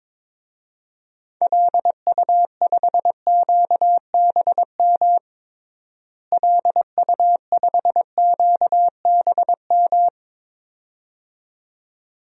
Morse code, heard as a keyed tone, about 22 wpm, 700 Hz. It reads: LU5QBM LU5QBM
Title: DXCC-100 Call Signs - 6 Characters - Repeated 2x - 22wpm